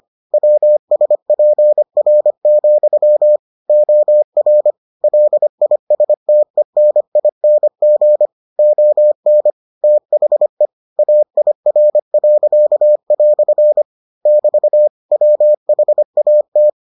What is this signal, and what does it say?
Morse code, heard as a keyed tone, about 25 wpm, 600 Hz. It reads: WSPR, OR LISTENING ON THE AIR." <BT> WHAT